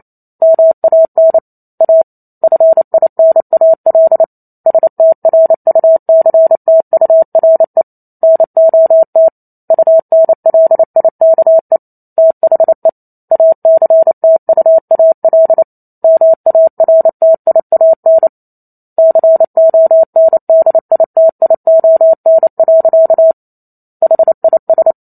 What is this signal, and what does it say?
Morse code, heard as a keyed tone, about 28 wpm, 650 Hz. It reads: MAN A FINAL STRUCTURE NOT UNLIKE THE ACTUAL MARTIAN CONDITION. HIS